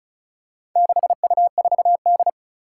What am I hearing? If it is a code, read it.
Morse code, heard as a keyed tone, about 36 wpm, 700 Hz. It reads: 6U4D
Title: Four Challenging Letter-Number Combinations 35wpm